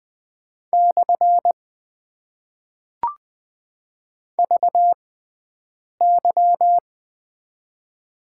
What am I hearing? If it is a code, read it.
Morse code, heard as a keyed tone, about 20 wpm, spaced as for 7 wpm, 700 Hz. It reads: / VY
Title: Top 90 QSO Elements - 6x Character Spacing - 20wpm